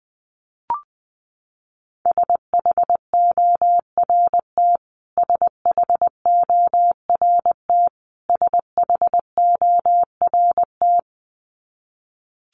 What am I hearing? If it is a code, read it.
Morse code, heard as a keyed tone, about 20 wpm, 700 Hz. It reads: SHORT SHORT SHORT
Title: Top 500 Words Repeated 20wpm